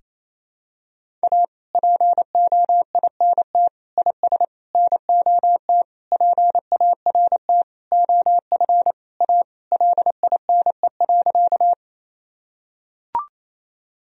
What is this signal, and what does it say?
Morse code, heard as a keyed tone, about 28 wpm, 700 Hz. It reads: A POINT IS NOT PART OF A LINE.
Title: Sentences from Top 1000 Words - Review Part 1 - 28wpm